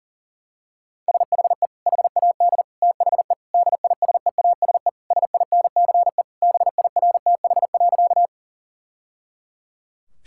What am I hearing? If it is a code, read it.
Morse code, heard as a keyed tone, about 37 wpm, 700 Hz. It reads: SHE HAD THE DISEASE SINCE BIRTH.
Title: Sentences from Top 1000 Words - Review Part 6 - 40wpm